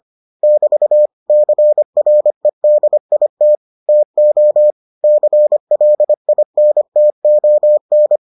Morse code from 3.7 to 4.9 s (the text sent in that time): TO